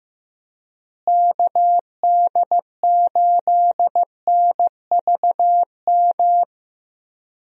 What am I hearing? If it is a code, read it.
Morse code, heard as a keyed tone, about 15 wpm, 700 Hz. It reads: KD8NVM
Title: USA Call Signs - 6 Characters - Repeated 2x - 15wpm